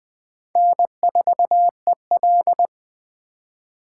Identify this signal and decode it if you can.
Morse code, heard as a keyed tone, about 20 wpm, 700 Hz. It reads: N4EL